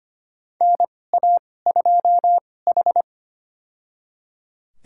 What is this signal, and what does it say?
Morse code, heard as a keyed tone, about 25 wpm, spaced as for 18 wpm, 700 Hz. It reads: NA2H
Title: USA Call Signs - 4 Characters - 2x Character Spacing - 25wpm